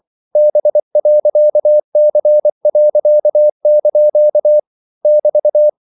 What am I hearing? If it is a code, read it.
Morse code, heard as a keyed tone, about 24 wpm, 600 Hz. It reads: B.C.) <BT>
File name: CW / Practice / EnglishQuotations0224